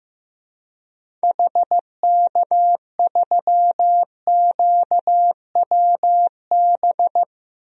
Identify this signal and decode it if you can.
Morse code, heard as a keyed tone, about 15 wpm, 700 Hz. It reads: HK3QWB
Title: DXCC-100 Call Signs - 6 Characters - Repeated 2x - 15wpm